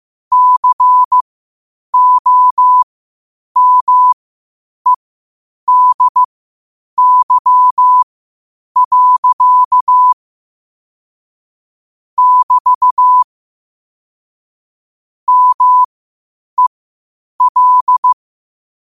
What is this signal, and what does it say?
Morse code, heard as a keyed tone, about 15 wpm, spaced as for 8 wpm, 1000 Hz. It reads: COMEDY. <BT> MEL
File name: qotdfiles / 8wpm